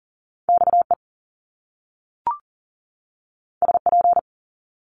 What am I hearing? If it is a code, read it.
Morse code, heard as a keyed tone, about 40 wpm, 700 Hz. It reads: XE SP
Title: Top Two Letter Combinations in Words 40wpm